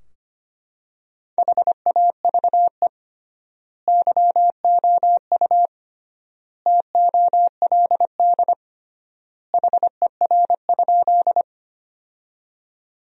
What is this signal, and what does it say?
Morse code, heard as a keyed tone, about 25 wpm, 700 Hz. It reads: HAVE YOU TOLD HER?
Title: 3x Word-Spacing - Sentences from Top 400 Words - 25wpm